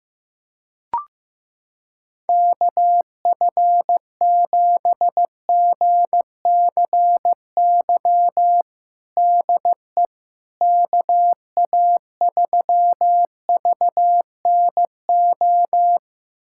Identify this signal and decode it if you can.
Morse code, heard as a keyed tone, about 16 wpm, 700 Hz. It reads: KF7GCY DE KA3VNO
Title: USA Callsign DE Callsign - 6 Characters - 15wpm